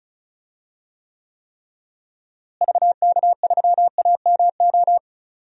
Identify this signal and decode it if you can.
Morse code, heard as a keyed tone, about 34 wpm, 700 Hz. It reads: VK3AMO
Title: DXCC-100 Call Signs - 6 Characters - Repeated 3x - 35wpm